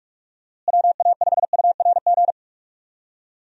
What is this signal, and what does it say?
Morse code, heard as a keyed tone, about 46 wpm, 700 Hz. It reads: WA5URG
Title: USA Call Signs - 6 Characters - Repeated 2x - 45wpm